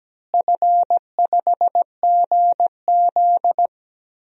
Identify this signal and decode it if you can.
Morse code, heard as a keyed tone, about 17 wpm, 700 Hz. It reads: F5GZ
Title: DXCC-100 Call Signs - 4 Characters - Repeated 2x - 17wpm